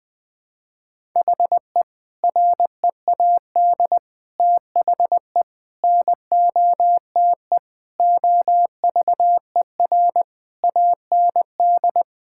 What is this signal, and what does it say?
Morse code, heard as a keyed tone, about 20 wpm, 700 Hz. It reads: HE READ THE NOTE OVER AND
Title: Sentences with 4-Word Phrases - 20wpm